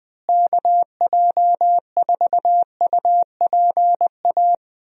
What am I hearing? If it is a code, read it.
Morse code, heard as a keyed tone, about 20 wpm, 700 Hz. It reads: KJ4UPA
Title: USA Call Signs - 6 Characters - Repeated 2x - 20wpm